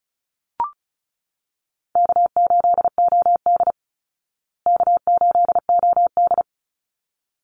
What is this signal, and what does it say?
Morse code, heard as a keyed tone, about 34 wpm, 700 Hz. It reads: K8OD K8OD
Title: Call Signs Repeated 3x - 35wpm